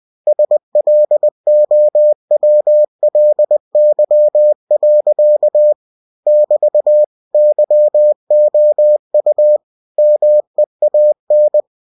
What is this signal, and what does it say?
Morse code, heard as a keyed tone, about 20 wpm, 600 Hz. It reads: SLOWLY. <BT> YOU MEAN
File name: cw-books / winnie-the-pooh / winnie-20wpm-page-0105